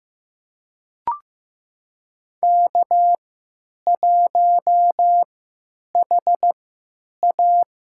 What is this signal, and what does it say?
Morse code, heard as a keyed tone, about 15 wpm, spaced as for 9 wpm, 700 Hz. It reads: K1HA
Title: USA Call Signs - 4 Characters - 3x Character Spacing - 15wpm